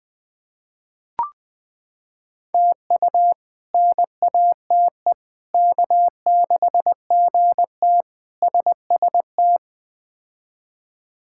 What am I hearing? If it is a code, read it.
Morse code, heard as a keyed tone, about 21 wpm, 700 Hz. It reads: TU NATE K6GT SST
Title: K1USN SST QSO Practice - 20wpm